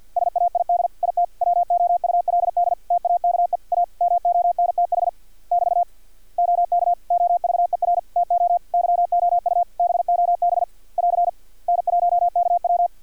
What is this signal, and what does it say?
Morse code, heard as a keyed tone, about 50 wpm, 700 Hz. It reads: SAID IT WOULD TAKE A MONTH <BT> OK OVER TO YOU BOB <AR> N1KW
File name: archived / CRSnet